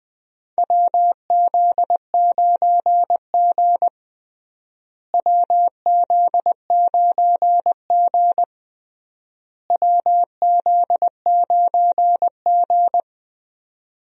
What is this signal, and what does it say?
Morse code, heard as a keyed tone, about 20 wpm, 700 Hz. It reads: WZ9G WZ9G WZ9G
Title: USA Call Signs - 4 Characters - Repeated 3x - 20wpm